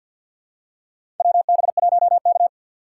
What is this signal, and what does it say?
Morse code, heard as a keyed tone, about 50 wpm, 700 Hz. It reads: WB1K
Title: USA Call Signs - 4 Characters - 2x Character Spacing - 50wpm